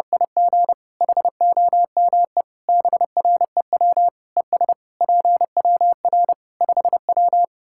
Morse code, heard as a keyed tone, about 30 wpm, 700 Hz. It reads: IG HOME BREW ES PWR 5W
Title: Practice QSO - Exchange 2 - Info - 30wpm